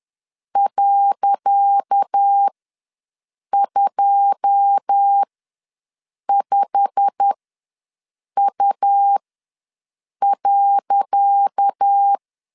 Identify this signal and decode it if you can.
Morse code, heard as a keyed tone, about 10 wpm, spaced as for 6 wpm, 800 Hz. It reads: .25U.
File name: LEZIONI / 10 SIGNS / 10C DOT ws 10wpm 4213